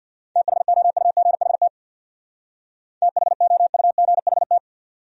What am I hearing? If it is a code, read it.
Morse code, heard as a keyed tone, about 64 wpm, 700 Hz. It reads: THOUGHT THOUGHT